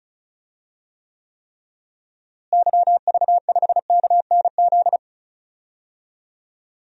Morse code, heard as a keyed tone, about 34 wpm, 700 Hz. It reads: YV5KNZ
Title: DXCC-100 Call Signs - 6 Characters - Repeated 3x - 35wpm